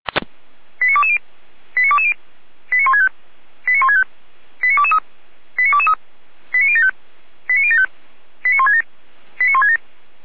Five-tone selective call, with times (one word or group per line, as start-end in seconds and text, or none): 0.8-1.2 89220
1.8-2.1 89220
2.7-3.1 89165
3.7-4.0 89165
4.6-5.0 89223
5.6-5.9 89223
6.5-6.9 89075
7.5-7.9 89075
8.4-8.8 89167
9.4-9.8 89167